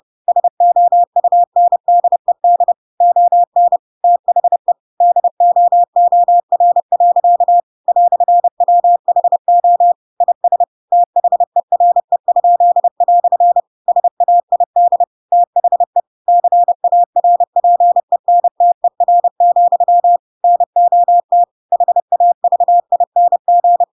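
Morse code, heard as a keyed tone, about 30 wpm, 700 Hz. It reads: SOUNDED ON THE DOOR. "WHO IS THERE?" SAID THE CARPENTER, NOT HAVING
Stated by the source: Chapter 2 of The Adventures of Pinocchio read as a book in Morse code (CW) at 30 wpm, by ik1hge.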